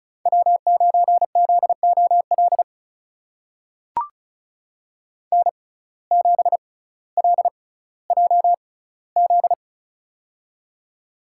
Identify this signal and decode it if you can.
Morse code, heard as a keyed tone, about 36 wpm, spaced as for 18 wpm, 700 Hz. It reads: W9ZOL N7LJZ
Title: USA Call Signs - 6x Character Spacing - 35wpm